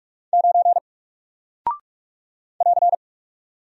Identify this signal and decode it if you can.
Morse code, heard as a keyed tone, about 44 wpm, 700 Hz. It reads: 9 <AR>
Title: Single Character (Letter:Number:Punctuation:Prosign) 45wpm